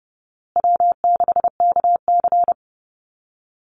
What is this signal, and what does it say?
Morse code, heard as a keyed tone, about 30 wpm, 700 Hz. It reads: W6KC
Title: Easy Call Signs Repeated 3x - 30wpm